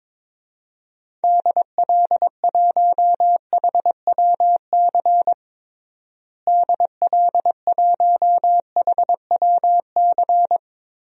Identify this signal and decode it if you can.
Morse code, heard as a keyed tone, about 22 wpm, 700 Hz. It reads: DL1HWC DL1HWC
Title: DXCC-100 Call Signs - 6 Characters - Repeated 3x - 22wpm